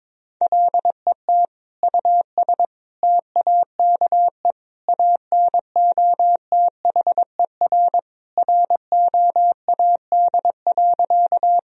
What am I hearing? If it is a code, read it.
Morse code, heard as a keyed tone, about 22 wpm, 700 Hz. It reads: LET US TAKE ANOTHER ROAD.
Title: Sentences from Top 1000 Words - Review Part 2 - 22wpm